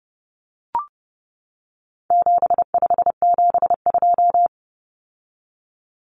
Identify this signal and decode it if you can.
Morse code, heard as a keyed tone, about 30 wpm, 700 Hz. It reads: 7572
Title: Numbers - 4-Digit 30wpm